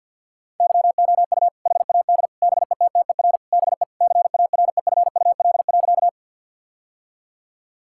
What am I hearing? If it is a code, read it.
Morse code, heard as a keyed tone, about 50 wpm, 700 Hz. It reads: YOU HAD BETTER BE CAREFUL.
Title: Sentences from Top 1600 Words 50wpm